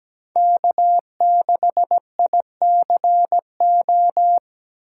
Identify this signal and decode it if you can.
Morse code, heard as a keyed tone, about 17 wpm, 700 Hz. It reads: K6ICO
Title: USA Call Signs - 5 Characters - 8x Character Spacing - 17wpm